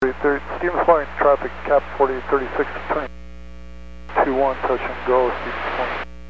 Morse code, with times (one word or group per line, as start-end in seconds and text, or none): none